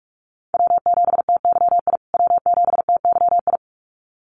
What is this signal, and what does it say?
Morse code, heard as a keyed tone, about 44 wpm, 700 Hz. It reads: W7TYI W7TYI
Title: USA Call Signs Repeated 2x 45wpm